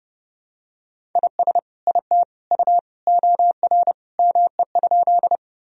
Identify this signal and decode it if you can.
Morse code, heard as a keyed tone, about 30 wpm, 700 Hz. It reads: IS IT U OR ME?
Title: Sentences from Top 200 Words and Abbreviations 30wpm